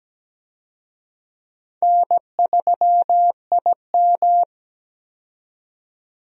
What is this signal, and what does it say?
Morse code, heard as a keyed tone, about 17 wpm, 700 Hz. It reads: N3IM
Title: USA Call Signs - 4 Characters - 17wpm